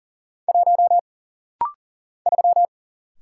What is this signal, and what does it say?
Morse code, heard as a keyed tone, about 40 wpm, 700 Hz. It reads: 1 3
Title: Single Letter-Number - Rapid-Fire 40wpm